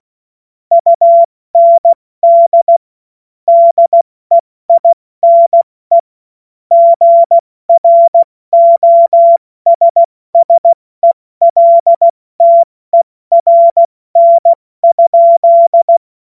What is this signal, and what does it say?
Morse code, heard as a keyed tone, about 16 wpm, 680 Hz. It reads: UND DEINE GROSSELTERN?
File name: Morsekurs / Witze / Witz 12. Oktober 2021, 70 F80